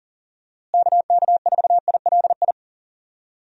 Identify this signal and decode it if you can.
Morse code, heard as a keyed tone, about 40 wpm, 700 Hz. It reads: KK4ILI
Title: USA Call Signs - 6 Characters - Repeated 3x - 40wpm